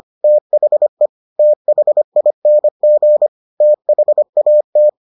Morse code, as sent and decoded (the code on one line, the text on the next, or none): - .... / - .... .. -. --. / - .... .- -
THE THING THAT